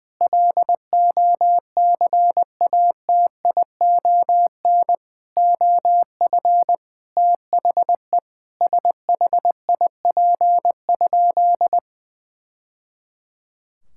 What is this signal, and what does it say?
Morse code, heard as a keyed tone, about 20 wpm, 700 Hz. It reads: LOCATION OF THE SHIP?
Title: Sentences from Top 1000 Words - Review Part 6 - 20wpm